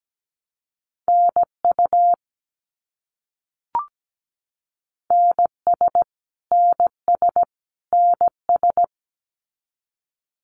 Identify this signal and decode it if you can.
Morse code, heard as a keyed tone, about 17 wpm, 700 Hz. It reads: NU NS NS NS
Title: Canadian Provinces and Territories - Repeated 3x - 17wpm